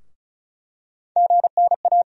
Morse code, heard as a keyed tone, about 34 wpm, 700 Hz. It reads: GNA
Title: USA Call Sign Postfixes - 3 Letters 35wpm